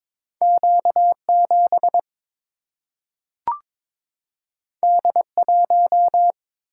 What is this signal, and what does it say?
Morse code, heard as a keyed tone, about 22 wpm, 700 Hz. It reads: Q7 D1
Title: Single-Letter Single-Number 22wpm